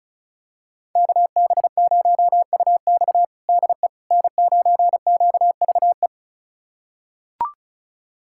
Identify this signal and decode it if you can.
Morse code, heard as a keyed tone, about 36 wpm, 700 Hz. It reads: KB0UX DE N9QVE
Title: USA Callsign DE Callsign 35wpm